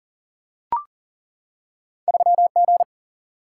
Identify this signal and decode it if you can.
Morse code, heard as a keyed tone, about 40 wpm, 700 Hz. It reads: 3G
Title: Single-Letter Single-Number 40wpm